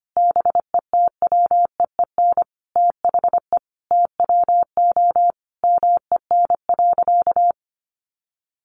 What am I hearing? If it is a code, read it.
Morse code, heard as a keyed tone, about 25 wpm, 700 Hz. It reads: BETWEEN THE TWO MEN.
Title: Sentences from Top 1300 Words 25wpm